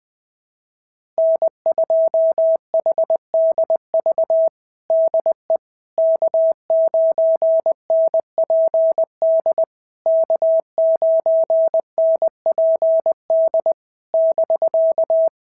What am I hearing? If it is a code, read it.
Morse code, heard as a keyed tone, about 20 wpm, 650 Hz. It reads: N2HDV DE K9NPD K9NPD <BK>
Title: SOTA QSO Practice 20wpm